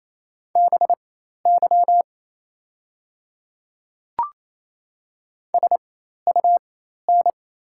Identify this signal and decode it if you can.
Morse code, heard as a keyed tone, about 28 wpm, spaced as for 13 wpm, 700 Hz. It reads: BY SUN
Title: 4x Character Spacing - Top 300 Words - 28wpm